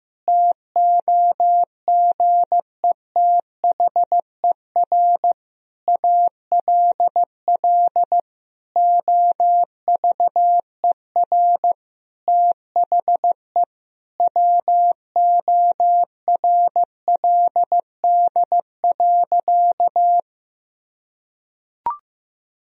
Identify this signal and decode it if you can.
Morse code, heard as a keyed tone, about 15 wpm, 700 Hz. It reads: TOGETHER ALL OVER THE WORLD.
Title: Sentences from Top 1000 Words - Review Part 2 - 15wpm